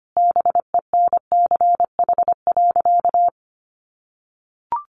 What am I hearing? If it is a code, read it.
Morse code, heard as a keyed tone, about 25 wpm, 700 Hz. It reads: BENCH.
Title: Sentences from Top 1400 Words 25wpm